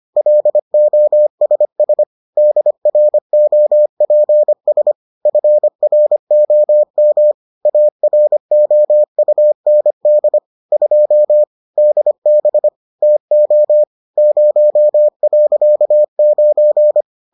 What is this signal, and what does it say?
Morse code, heard as a keyed tone, about 25 wpm, 600 Hz. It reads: LOSS DROPS FROM AROUND 2 DB TO 0.9